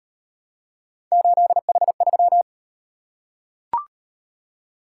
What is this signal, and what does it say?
Morse code, heard as a keyed tone, about 38 wpm, 700 Hz. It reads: 8H3